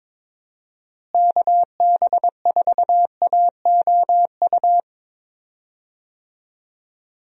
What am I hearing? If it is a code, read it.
Morse code, heard as a keyed tone, about 22 wpm, 700 Hz. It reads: KB4AOU
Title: USA Call Signs - 6 Characters - Repeated 3x - 22wpm